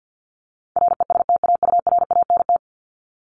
Morse code, heard as a keyed tone, about 50 wpm, 700 Hz. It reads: RESTAURANT
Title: Top 1751-2000 Words 50wpm